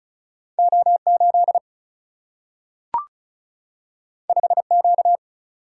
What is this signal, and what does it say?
Morse code, heard as a keyed tone, about 36 wpm, 700 Hz. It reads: O8 5Q
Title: Single-Letter Single-Number 35wpm